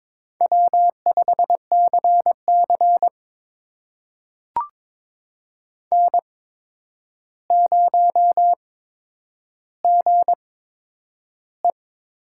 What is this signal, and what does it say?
Morse code, heard as a keyed tone, about 22 wpm, spaced as for 9 wpm, 700 Hz. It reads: W5CC N0GE
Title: USA Call Signs - 4 Characters - 8x Character Spacing - 22wpm